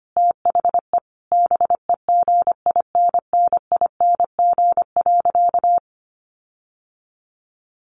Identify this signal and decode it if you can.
Morse code, heard as a keyed tone, about 25 wpm, 700 Hz. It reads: THE BEGINNING.
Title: Sentences from Top 1500 Words 25wpm